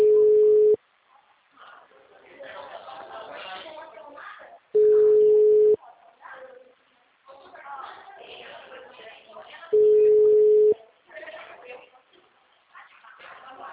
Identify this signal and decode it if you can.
Ringback tone (Europe, 425 Hz), with speech.